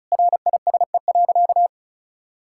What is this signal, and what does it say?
Morse code, heard as a keyed tone, about 36 wpm, 700 Hz. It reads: RISE.
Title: Sentences from Top 1000 Words - Review Part 6 - 35wpm